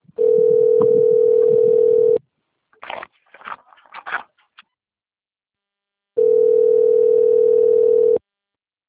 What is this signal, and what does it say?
Ringback tone (North America), 440 + 480 Hz.